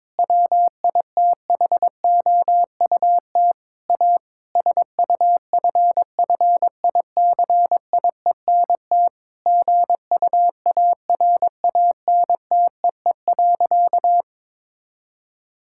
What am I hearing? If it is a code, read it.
Morse code, heard as a keyed tone, about 22 wpm, 700 Hz. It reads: WITHOUT A SUFFICIENT GUARANTEE.
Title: Sentences from Top 1600 Words 22wpm